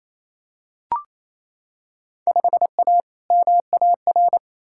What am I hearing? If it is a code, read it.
Morse code, heard as a keyed tone, about 27 wpm, 700 Hz. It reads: 5A MAR
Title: Field Day Exchanges Encore - 28wpm